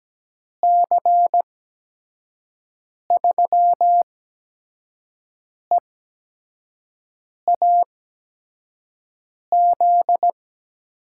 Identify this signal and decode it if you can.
Morse code, heard as a keyed tone, about 17 wpm, spaced as for 5 wpm, 700 Hz. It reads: C3EAZ
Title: USA Call Signs - 6 Characters - 8x Character Spacing - 17wpm